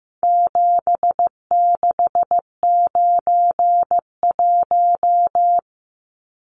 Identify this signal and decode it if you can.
Morse code, heard as a keyed tone, about 15 wpm, 700 Hz. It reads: 7691
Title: Numbers - 4-Digit 15wpm